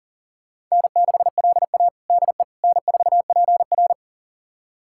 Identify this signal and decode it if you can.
Morse code, heard as a keyed tone, about 37 wpm, 700 Hz. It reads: N6LA DE N4PR
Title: USA Callsign DE Callsign - 4 Characters - 40wpm